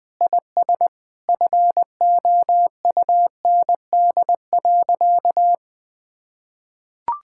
Morse code, heard as a keyed tone, about 20 wpm, 700 Hz. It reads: IS FOUND.
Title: Sentences with 9-Word Phrases - Spoken Beginning - 20wpm